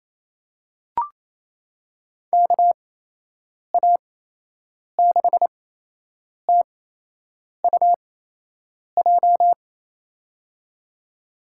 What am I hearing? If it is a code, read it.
Morse code, heard as a keyed tone, about 28 wpm, spaced as for 8 wpm, 700 Hz. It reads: KA6TUJ
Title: USA Call Signs - 8x Character Spacing - 28wpm